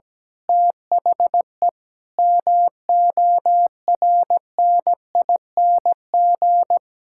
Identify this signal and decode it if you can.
Morse code, heard as a keyed tone, about 17 wpm, 700 Hz. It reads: THE MORNING
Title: Pangram Challenge v2 - 17wpm